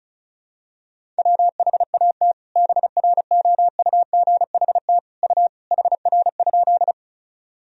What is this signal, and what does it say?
Morse code, heard as a keyed tone, about 36 wpm, 700 Hz. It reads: WHAT BROUGHT U HR?
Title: Sentences from Top 500 Words and Abbreviations 35wpm